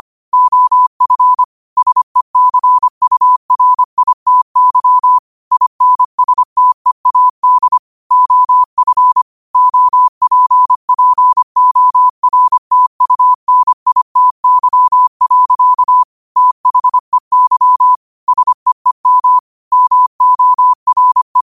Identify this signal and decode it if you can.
Morse code, heard as a keyed tone, about 25 wpm, 1000 Hz. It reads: OF SECURITY INSTEAD OF OPPORTUNITY. THEY SEEM MORE
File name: qotdfiles / 25wpm